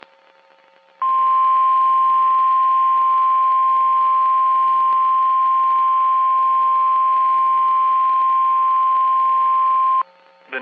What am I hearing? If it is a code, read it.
Emergency Alert System (SAME) tones (the 1050 Hz attention tone), with speech. It is heard from 1.0 to 10.1 s.
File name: eas messages / EAS WXM31 Elmira